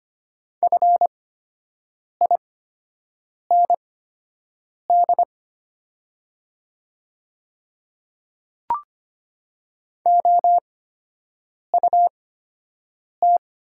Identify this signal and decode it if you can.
Morse code, heard as a keyed tone, about 25 wpm, spaced as for 7 wpm, 700 Hz. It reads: FIND OUT
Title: 8x Character Spacing - Top 100 Words - 25wpm